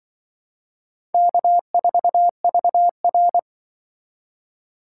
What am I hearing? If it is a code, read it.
Morse code, heard as a keyed tone, about 23 wpm, 700 Hz. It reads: K4VR
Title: Speed Racing - USA 1x2 Call Signs 20wpm